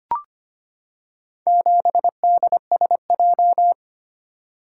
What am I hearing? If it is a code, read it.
Morse code, heard as a keyed tone, about 25 wpm, 700 Hz. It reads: 7DSJ
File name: practice / 25wpm / Four Letter-Number Combinations - 25wpm